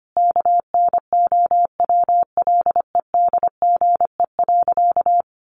Morse code, heard as a keyed tone, about 25 wpm, 700 Hz. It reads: KNOWLEDGE.
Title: Sentences from Top 1200 Words 25wpm